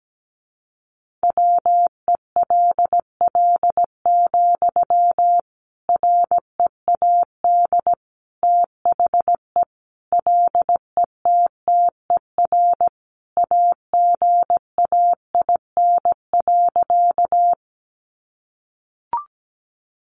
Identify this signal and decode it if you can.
Morse code, heard as a keyed tone, about 17 wpm, 700 Hz. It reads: WELL, READ THE LETTER AGAIN.
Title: Sentences from Top 300 Words 17wpm